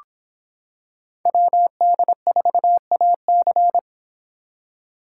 Morse code, heard as a keyed tone, about 26 wpm, 700 Hz. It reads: WD4AC